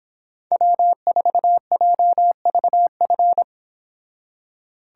Morse code, heard as a keyed tone, about 26 wpm, 700 Hz. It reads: W4JVF